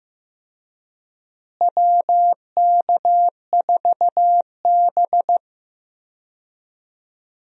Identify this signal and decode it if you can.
Morse code, heard as a keyed tone, about 15 wpm, 700 Hz. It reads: WK4B
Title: USA Call Signs - 4 Characters - Repeated 3x - 15wpm